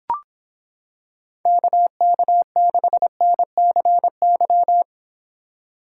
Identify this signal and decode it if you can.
Morse code, heard as a keyed tone, about 26 wpm, 700 Hz. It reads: KK6NCY